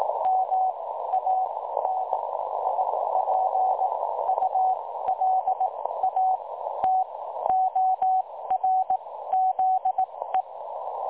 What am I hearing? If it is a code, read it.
Morse code, heard as a keyed tone, about 18 wpm, spaced as for 13 wpm, 750 Hz, with noise. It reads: MANIPULATORZE
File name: dlazaawansowanych / nagraniazpasma / websdr recording 3565.0kHz